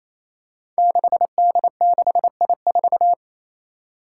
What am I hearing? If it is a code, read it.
Morse code, heard as a keyed tone, about 28 wpm, 700 Hz. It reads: 6D6I4
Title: Five Challenging Letter-Number Combinations 28wpm